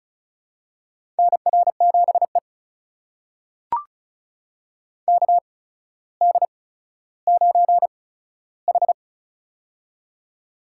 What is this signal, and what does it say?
Morse code, heard as a keyed tone, about 34 wpm, spaced as for 15 wpm, 700 Hz. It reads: NR7E KD9H
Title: USA Call Signs - 4 Characters - 8x Character Spacing - 35wpm